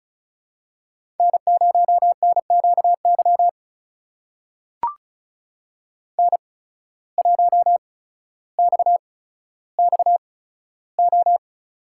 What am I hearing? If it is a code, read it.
Morse code, heard as a keyed tone, about 36 wpm, spaced as for 15 wpm, 700 Hz. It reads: N0NQY N1XXO
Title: USA Call Signs - 5 Characters - 8x Character Spacing - 35wpm